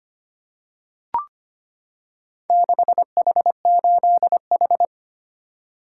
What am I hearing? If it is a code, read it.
Morse code, heard as a keyed tone, about 25 wpm, 700 Hz. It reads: 6H8H
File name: practice / 25wpm / Four Letter-Number Combinations - 25wpm